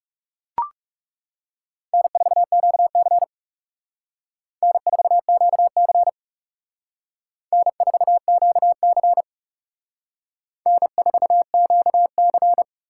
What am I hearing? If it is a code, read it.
Morse code, heard as a keyed tone, about 37 wpm, 700 Hz. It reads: N4QC N4QC N4QC N4QC
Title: Speed Racing - USA 1x2 Call Signs 30wpm